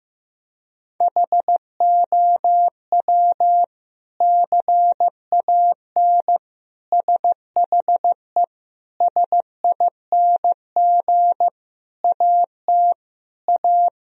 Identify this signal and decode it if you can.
Morse code, heard as a keyed tone, about 15 wpm, 700 Hz. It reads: HOW CAN SHE SING AT A
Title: Sentences from Top 1000 Words - Review Part 2 - 15wpm